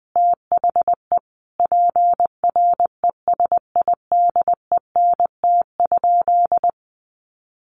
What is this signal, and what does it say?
Morse code, heard as a keyed tone, about 19 wpm, 700 Hz. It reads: THE PRESIDENT?
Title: Sentences from Top 1300 Words 20wpm